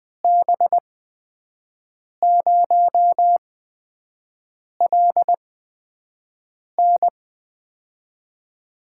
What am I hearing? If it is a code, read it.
Morse code, heard as a keyed tone, about 20 wpm, spaced as for 5 wpm, 700 Hz. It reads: B0LN
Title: USA Call Signs - 8x Character Spacing - 20wpm